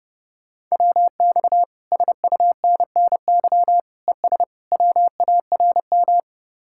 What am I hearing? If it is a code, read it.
Morse code, heard as a keyed tone, about 37 wpm, 700 Hz. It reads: WX SUNNY ES WARM
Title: Speed-Racing Common QSO Phrases - 30wpm